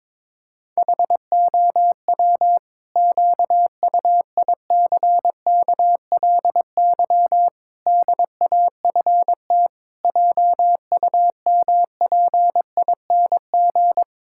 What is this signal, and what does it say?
Morse code, heard as a keyed tone, about 22 wpm, 700 Hz. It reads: HOW QUICKLY DAFT JUMPING
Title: Pangram Challenge v2 - 22wpm